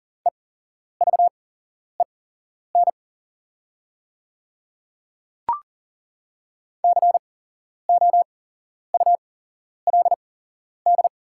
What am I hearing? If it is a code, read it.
Morse code, heard as a keyed tone, about 39 wpm, spaced as for 11 wpm, 700 Hz. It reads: EVEN COULD
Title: 8x Character Spacing - Top 300 Words - 40wpm